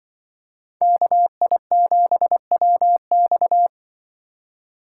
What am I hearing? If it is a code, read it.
Morse code, heard as a keyed tone, about 24 wpm, 700 Hz. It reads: KI7WX
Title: Speed Racing - USA 2x2 Call Signs 20wpm